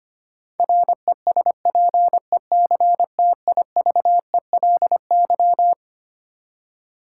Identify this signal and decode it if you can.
Morse code, heard as a keyed tone, about 25 wpm, 700 Hz. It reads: RESPECTIVELY
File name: Twelve Letter Words - 25wpm